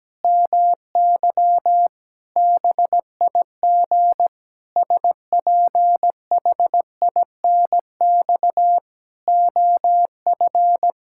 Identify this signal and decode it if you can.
Morse code, heard as a keyed tone, about 17 wpm, 700 Hz. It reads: MY BIG SPHINX OF
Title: Pangram Challenge v2 - 17wpm